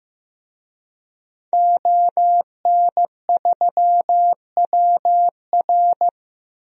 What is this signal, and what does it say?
Morse code, heard as a keyed tone, about 16 wpm, 700 Hz. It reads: ON3WR
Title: DXCC-100 Call Signs - 5 Characters - Repeated 3x - 15wpm